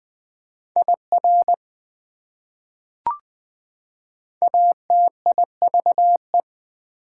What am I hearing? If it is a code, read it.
Morse code, heard as a keyed tone, about 20 wpm, 700 Hz. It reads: IR ATIVE
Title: Prefixes and Suffixes 20wpm